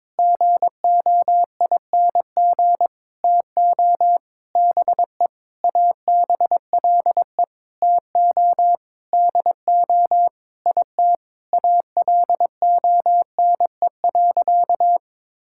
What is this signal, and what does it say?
Morse code, heard as a keyed tone, about 21 wpm, 700 Hz. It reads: GOING TO BE ABLE TO DO IT ALONE.
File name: Sentences with 7-Word Phrases - 22wpm